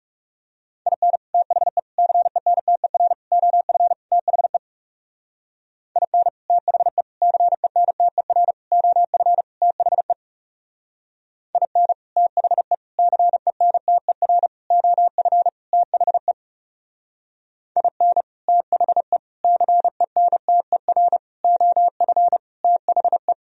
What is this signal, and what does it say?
Morse code, heard as a keyed tone, about 36 wpm, 700 Hz. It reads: IN THE CENTER OF THE IN THE CENTER OF THE IN THE CENTER OF THE IN THE CENTER OF THE
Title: Speed-Racing - Sets of 5 Words 30wpm